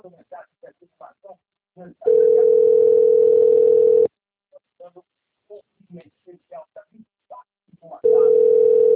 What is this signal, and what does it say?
Ringback tone (North America), 440 + 480 Hz, with speech.
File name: rg-1600-unknown- - - .10270